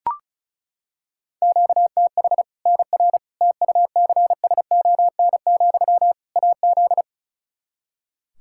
Morse code, heard as a keyed tone, about 36 wpm, 700 Hz. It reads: QTH NR TUCSON, AZ
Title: K1USN SST QSO Practice 35wpm